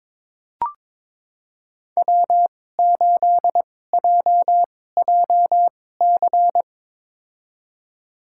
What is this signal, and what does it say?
Morse code, heard as a keyed tone, about 21 wpm, spaced as for 16 wpm, 690 Hz. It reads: W8JJC
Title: USA Call Signs - 5 Characters - 2x Character Spacing - 22wpm